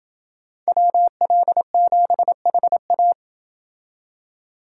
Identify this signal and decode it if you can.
Morse code, heard as a keyed tone, about 28 wpm, 700 Hz. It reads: WL7HA